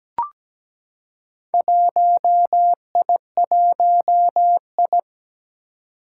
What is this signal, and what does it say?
Morse code, heard as a keyed tone, about 17 wpm, 700 Hz. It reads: 1I1I
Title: Four Challenging Letter-Number Combinations 17wpm